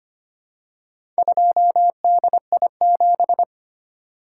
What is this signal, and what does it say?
Morse code, heard as a keyed tone, about 25 wpm, 700 Hz. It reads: 2DI7
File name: practice / 25wpm / Four Letter-Number Combinations - 25wpm